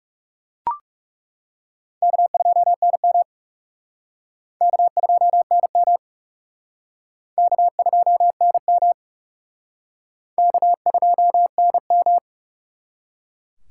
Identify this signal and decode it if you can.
Morse code, heard as a keyed tone, about 36 wpm, 700 Hz. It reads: K2NM K2NM K2NM K2NM
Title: Speed Racing - USA 1x2 Call Signs 30wpm